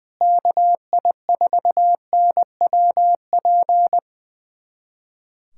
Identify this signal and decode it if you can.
Morse code, heard as a keyed tone, about 21 wpm, 700 Hz. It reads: KI4NWP
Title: USA Callsign DE Callsign 20wpm